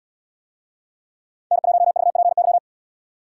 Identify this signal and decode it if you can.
Morse code, heard as a keyed tone, about 70 wpm, 700 Hz. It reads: N1VPY